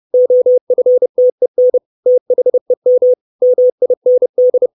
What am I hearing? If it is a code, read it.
Morse code, heard as a keyed tone, about 30 wpm, 500 Hz. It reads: OFTEN THEM MIND
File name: 632 Words 30WPM